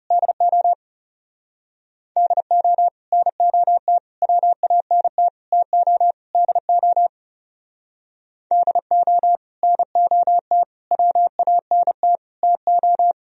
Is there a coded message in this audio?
Yes: Morse code from 0.1 to 13.1 s, heard as a keyed tone, about 38 wpm, 700 Hz. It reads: DO DO NOT WANT TO DO DO NOT WANT TO